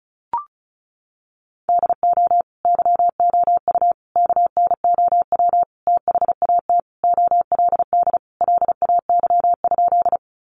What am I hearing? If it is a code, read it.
Morse code, heard as a keyed tone, about 34 wpm, 700 Hz. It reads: DO YOU KNOW THAT OLD LAY?
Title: Sentences from Top 400 Words 35wpm